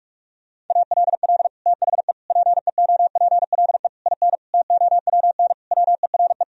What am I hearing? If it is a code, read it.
Morse code, heard as a keyed tone, about 48 wpm, 700 Hz. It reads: ALL THE PEOPLE IN TOWN WERE
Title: Sentences from Top 1000 Words - Review Part 4 - 45wpm